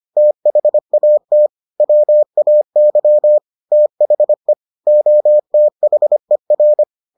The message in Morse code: THAT WAY THE OTHER

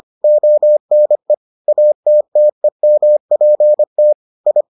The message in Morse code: ONE ATTEMPT I

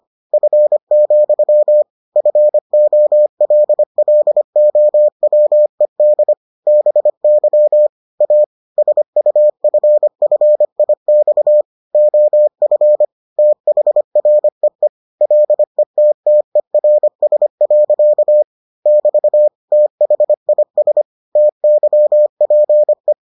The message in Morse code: F, FOLLOWED BY A SUFFIX OF THREE LETTERS. <BT> THIS TYPE

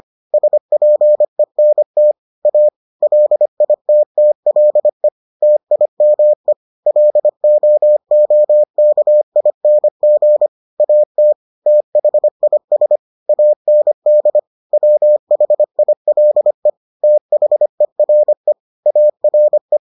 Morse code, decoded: SPENT A LITTLE TIME LOOKING AT THIS AND WHILE THERE ARE

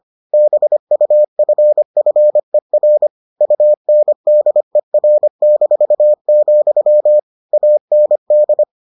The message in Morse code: BUFFER UNDER-, AND

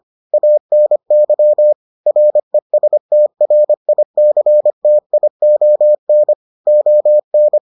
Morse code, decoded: ANY RESTRICTION ON

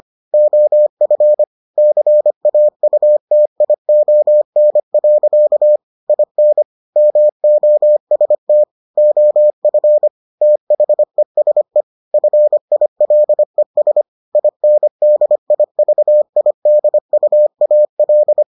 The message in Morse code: OF CAUTION. IN MOST OF THESE FILES INDIVIDUAL

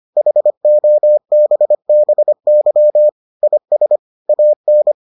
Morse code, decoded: HOBBY IS AN

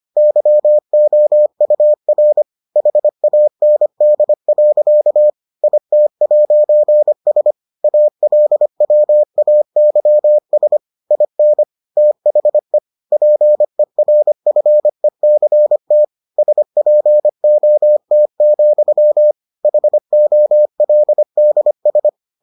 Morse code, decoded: YOUR HAND. IT'S ALWAYS IN THE PERFECT SPOT, HOLDS

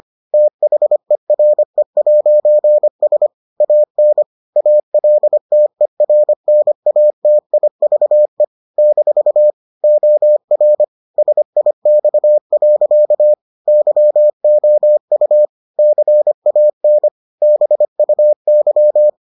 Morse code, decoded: THERE'S AN ALTERNATIVE - OR SIX. YOU CAN BUY

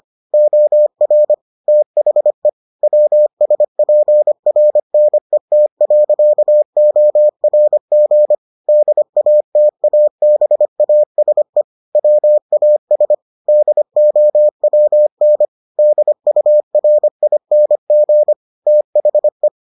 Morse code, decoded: OR THE WSPRNET.ORG DATABASE WAS DOWN DURING THE

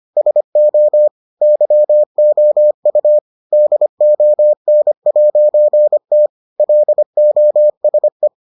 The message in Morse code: SO YOU DON'T LOSE

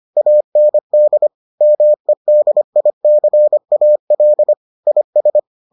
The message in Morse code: AND MEDICAL IS

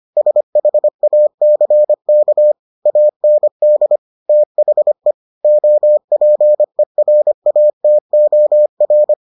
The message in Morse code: SHACK AND THE OPERATOR

